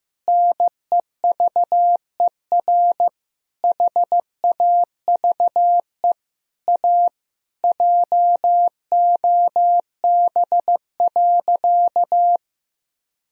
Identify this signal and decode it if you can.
Morse code, heard as a keyed tone, about 15 wpm, 700 Hz. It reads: NEVER HAVE A JOB.
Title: Sentences from Top 1000 Words - Review Part 3 - 15wpm